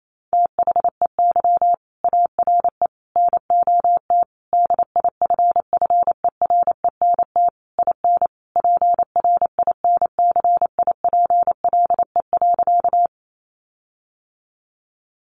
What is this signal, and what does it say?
Morse code, heard as a keyed tone, about 28 wpm, 700 Hz. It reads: THEY ARE NOT DIFFERENT IN PRINCIPLE.
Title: Sentences from Top 1400 Words 28wpm